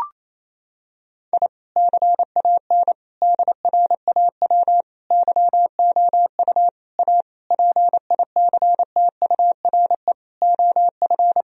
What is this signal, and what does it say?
Morse code, heard as a keyed tone, about 28 wpm, 700 Hz. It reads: I CAN DRAW YOU A PICTURE OF
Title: Sentences from Top 1000 Words - Review Part 1 - 28wpm